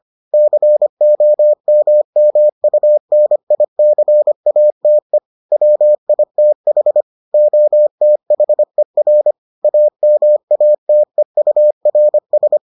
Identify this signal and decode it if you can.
Morse code, heard as a keyed tone, about 25 wpm, 600 Hz. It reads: COMMUNICATE WITH OTHER AMATEURS